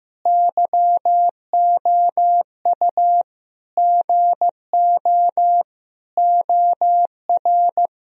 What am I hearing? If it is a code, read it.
Morse code, heard as a keyed tone, about 15 wpm, 700 Hz. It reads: YOU GO OR
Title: Sentences from Top 1000 Words - Review Part 3 - 15wpm